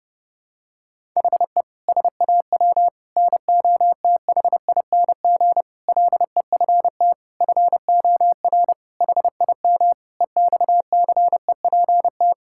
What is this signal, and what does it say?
Morse code, heard as a keyed tone, about 32 wpm, 700 Hz. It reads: HE SAW NOTHING LEFT FOR HIM EXCEPT
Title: Sentences from Top 1600 Words 30wpm